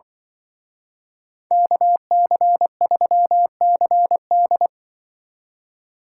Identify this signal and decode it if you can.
Morse code, heard as a keyed tone, about 25 wpm, 700 Hz. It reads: KC3CD